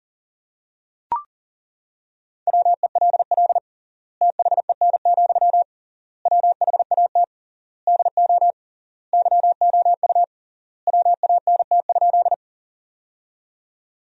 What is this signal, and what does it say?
Morse code, heard as a keyed tone, about 37 wpm, 700 Hz. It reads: WELL THEN, WHAT DO YOU WANT?
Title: 3x Word-Spacing - Sentences from Top 200 Words - 40wpm